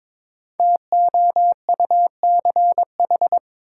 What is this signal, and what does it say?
Morse code, heard as a keyed tone, about 22 wpm, 700 Hz. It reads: TOUCH
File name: 5 Letter Words - First 3 Letters Given - 22wpm